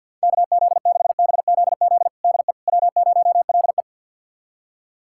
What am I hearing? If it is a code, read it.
Morse code, heard as a keyed tone, about 50 wpm, 700 Hz. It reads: KZ6BZZ DE W0LE
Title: USA Callsign DE Callsign 50wpm